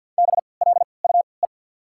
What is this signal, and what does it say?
Morse code, heard as a keyed tone, about 49 wpm, spaced as for 28 wpm, 700 Hz. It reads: BLUE
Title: 3x Character Spacing - Top 500 Words - 50wpm